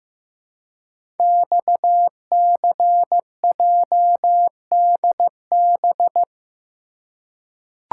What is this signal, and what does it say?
Morse code, heard as a keyed tone, about 15 wpm, 700 Hz. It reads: XCJDB
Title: Code Groups - 5 Letters - 15wpm